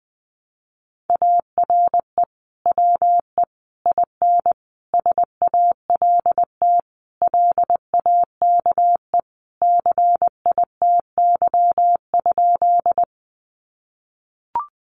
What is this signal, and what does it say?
Morse code, heard as a keyed tone, about 19 wpm, 700 Hz. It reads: ARE WE IN SALT LAKE CITY?
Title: Sentences from Top 1100 Words 20wpm